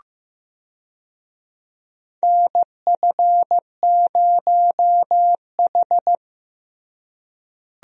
Morse code, heard as a keyed tone, about 15 wpm, 700 Hz. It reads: NF0H
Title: USA Call Signs - 4 Characters - Repeated 2x - 15wpm